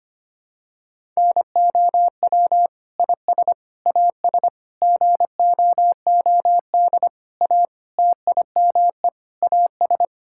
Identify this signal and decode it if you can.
Morse code, heard as a keyed tone, about 25 wpm, 700 Hz. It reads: NOW IS AS GOOD A TIME AS
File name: practice / 25wpm / Sentences with 9-Word Phrases - 25wpm